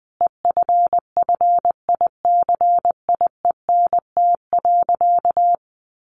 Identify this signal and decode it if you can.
Morse code, heard as a keyed tone, about 20 wpm, 700 Hz. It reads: EFFICIENT.
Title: Sentences from Top 1300 Words 20wpm